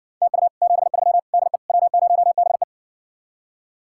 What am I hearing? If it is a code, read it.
Morse code, heard as a keyed tone, about 68 wpm, 700 Hz. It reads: TU 73 DE W0BE